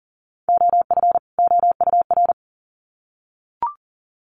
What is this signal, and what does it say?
Morse code, heard as a keyed tone, about 40 wpm, 700 Hz. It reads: OF OUR
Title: Sets of 2 Words 40wpm